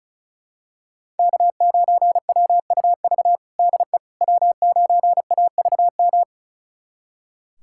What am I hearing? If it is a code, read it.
Morse code, heard as a keyed tone, about 36 wpm, 700 Hz. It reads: K9WUV DE W9AVM
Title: USA Callsign DE Callsign - 5 Characters - 35wpm